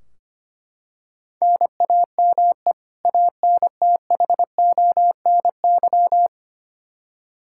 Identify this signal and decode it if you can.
Morse code, heard as a keyed tone, about 25 wpm, 700 Hz. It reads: NAME ANTHONY
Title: K1USN SST QSO Practice 25wpm